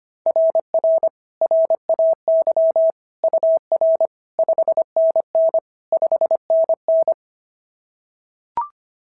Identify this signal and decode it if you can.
Morse code, heard as a keyed tone, about 22 wpm, 650 Hz. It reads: RR RAY UR 5NN 5NN
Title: SOTA QSO Practice 25wpm